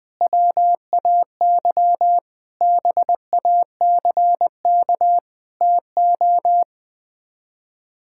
Morse code, heard as a keyed tone, about 20 wpm, 700 Hz. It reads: WAY BACK TO
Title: Sets of 8 Words - 20wpm